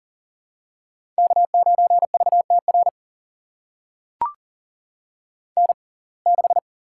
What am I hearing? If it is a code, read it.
Morse code, heard as a keyed tone, about 40 wpm, spaced as for 29 wpm, 700 Hz. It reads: K9VTR N6
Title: USA Call Signs - 5 Characters - 6x Character Spacing - 40wpm